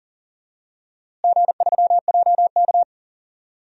Morse code, heard as a keyed tone, about 40 wpm, 700 Hz. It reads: G3JK
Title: DXCC-100 Call Signs - 4 Characters - Repeated 2x - 40wpm